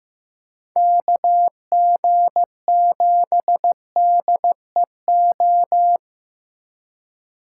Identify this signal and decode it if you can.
Morse code, heard as a keyed tone, about 18 wpm, 700 Hz. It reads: KG7DEO
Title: Speed Racing - USA 2x3 Call Signs 15wpm